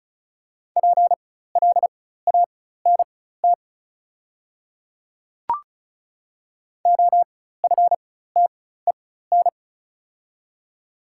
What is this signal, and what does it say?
Morse code, heard as a keyed tone, about 34 wpm, spaced as for 16 wpm, 700 Hz. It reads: PLANT OFTEN
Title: 4x Character Spacing - Top 300 Words - 35wpm